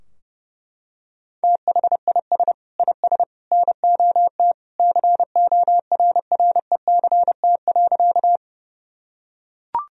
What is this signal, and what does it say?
Morse code, heard as a keyed tone, about 32 wpm, 700 Hz. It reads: THIS IS NOT CORRECT.
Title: Sentences from Top 1000 Words - Review Part 2 - 30wpm